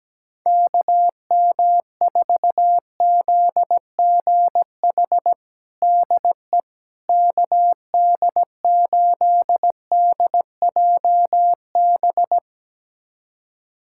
Morse code, heard as a keyed tone, about 17 wpm, 700 Hz. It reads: KM4ZGH DE KD8DJB
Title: USA Callsign DE Callsign 17wpm